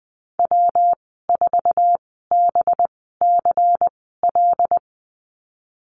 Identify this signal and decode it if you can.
Morse code, heard as a keyed tone, about 20 wpm, spaced as for 15 wpm, 700 Hz. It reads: W4BCL
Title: 186 Lesson 30g - Letter b - Call Signs 20wpm